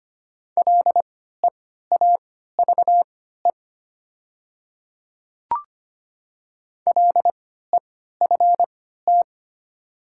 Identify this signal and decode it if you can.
Morse code, heard as a keyed tone, about 25 wpm, spaced as for 14 wpm, 700 Hz. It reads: LEAVE LEFT
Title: 3x Character Spacing - Top 500 Words - 25wpm